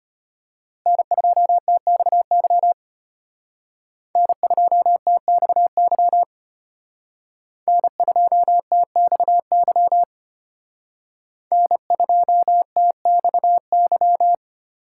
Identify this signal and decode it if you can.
Morse code, heard as a keyed tone, about 32 wpm, 700 Hz. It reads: N2TXY N2TXY N2TXY N2TXY